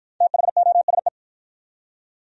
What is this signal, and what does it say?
Morse code, heard as a keyed tone, about 53 wpm, 700 Hz. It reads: THOSE